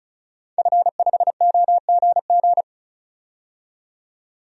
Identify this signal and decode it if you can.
Morse code, heard as a keyed tone, about 34 wpm, 700 Hz. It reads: F5OGG
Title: DXCC-100 Call Signs - 5 Characters - 35wpm